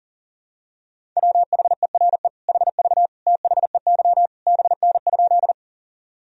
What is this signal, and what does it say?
Morse code, heard as a keyed tone, about 37 wpm, 700 Hz. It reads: WHERE HV THEY BN?
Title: Sentences from Top 200 Words and Abbreviations 40wpm